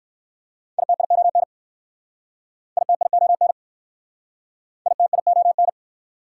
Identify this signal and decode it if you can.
Morse code, heard as a keyed tone, about 60 wpm, 700 Hz. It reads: ITION ITION ITION